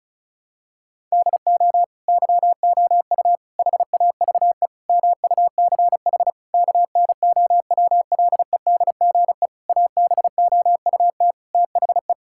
Morse code, heard as a keyed tone, about 36 wpm, 700 Hz. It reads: DO YOU HAVE MUCH KNOWLEDGE ABOUT THE
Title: Sentences from Top 1000 Words - Review Part 6 - 35wpm